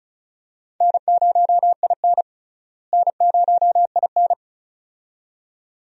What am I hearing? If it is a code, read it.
Morse code, heard as a keyed tone, about 36 wpm, 700 Hz. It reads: N0IN N0IN
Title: USA Call Signs - 4 Characters - Repeated 2x - 35wpm